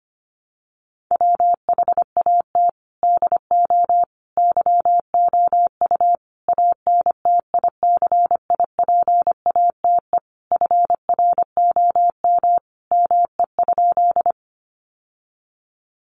Morse code, heard as a keyed tone, about 25 wpm, 700 Hz. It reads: WHAT DO YOU ANTICIPATE FROM ME?
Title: Sentences from Top 1500 Words 25wpm